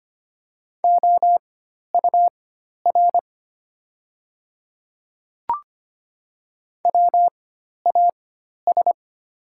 Morse code, heard as a keyed tone, about 25 wpm, spaced as for 12 wpm, 700 Hz. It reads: OUR WAS
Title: 4x Character Spacing - Top 200 Words - 25wpm